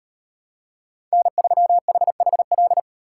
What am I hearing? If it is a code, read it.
Morse code, heard as a keyed tone, about 38 wpm, 700 Hz. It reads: N3HHL